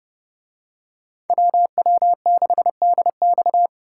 Morse code, heard as a keyed tone, about 29 wpm, 700 Hz. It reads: WW6DX